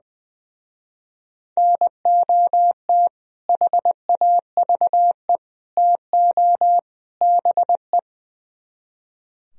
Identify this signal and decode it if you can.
Morse code, heard as a keyed tone, about 20 wpm, 700 Hz. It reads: NOT HAVE TO BE
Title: Speed-Racing - Sets of 4 Words 20wpm